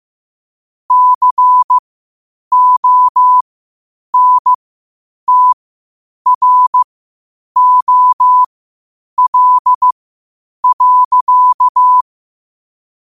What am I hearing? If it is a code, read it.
Morse code, heard as a keyed tone, about 15 wpm, spaced as for 8 wpm, 1000 Hz. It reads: CONTROL.